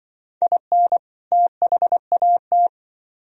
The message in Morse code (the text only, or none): IN THAT